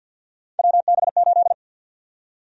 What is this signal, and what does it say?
Morse code, heard as a keyed tone, about 47 wpm, 700 Hz. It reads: WB8
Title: USA Call Sign Prefixes - Repeated 2x - 50wpm